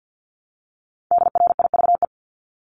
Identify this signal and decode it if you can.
Morse code, heard as a keyed tone, about 50 wpm, 700 Hz. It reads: DRIVE
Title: Multi-letter Phonemes Repeated 3x 50wpm